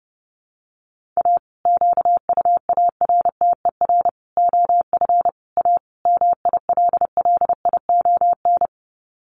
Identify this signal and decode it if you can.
Morse code, heard as a keyed tone, about 30 wpm, 700 Hz. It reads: A QUARTER OF A MILLION
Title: Sets of 5 Words Encore2 30wpm